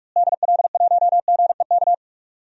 Morse code, heard as a keyed tone, about 43 wpm, 700 Hz. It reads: DL1GEK
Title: DXCC-100 Call Signs - 6 Characters - Repeated 2x - 45wpm